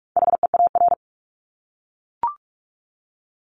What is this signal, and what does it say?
Morse code, heard as a keyed tone, about 44 wpm, 700 Hz. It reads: HEAR
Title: Top 500 Words Repeated 45wpm